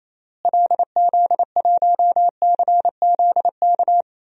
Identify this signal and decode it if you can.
Morse code, heard as a keyed tone, about 28 wpm, 700 Hz. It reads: LZ1CZK
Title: DXCC-100 Call Signs - 6 Characters - Repeated 3x - 28wpm